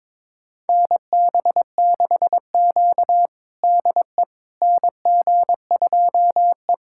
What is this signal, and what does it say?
Morse code, heard as a keyed tone, about 22 wpm, 700 Hz. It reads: NB6Q DE NG2E
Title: USA Callsign DE Callsign - 4 Characters - 22wpm